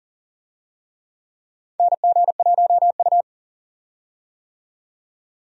Morse code, heard as a keyed tone, about 37 wpm, 700 Hz. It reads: NG1U
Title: USA Call Signs - 4 Characters - 40wpm